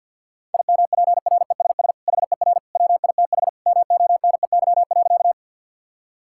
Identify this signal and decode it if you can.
Morse code, heard as a keyed tone, about 50 wpm, 700 Hz. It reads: IMPRESS HER WITH MONEY.
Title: Sentences from Top 1600 Words 50wpm